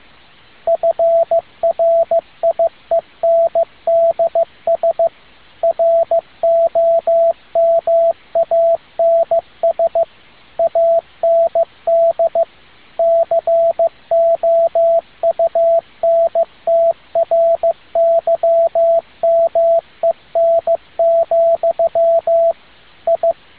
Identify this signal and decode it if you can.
Morse code, heard as a keyed tone, about 15 wpm, 670 Hz, with radio noise. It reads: FRIENDS ROMANS AND COUNTRYMEN, I